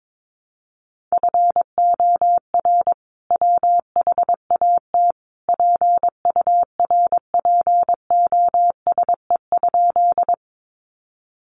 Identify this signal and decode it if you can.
Morse code, heard as a keyed tone, about 22 wpm, 700 Hz. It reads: FOR WHAT PURPOSE?
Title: Sentences from Top 800 Words 22wpm